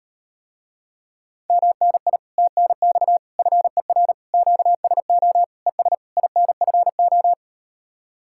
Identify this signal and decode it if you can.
Morse code, heard as a keyed tone, about 38 wpm, 700 Hz. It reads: MNI TNX FER QSO ES INFO